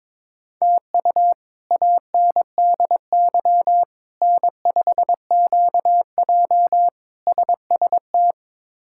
Morse code, heard as a keyed tone, about 21 wpm, 700 Hz. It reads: TU ANDY N5QJ SST
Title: K1USN SST QSO Practice - 22wpm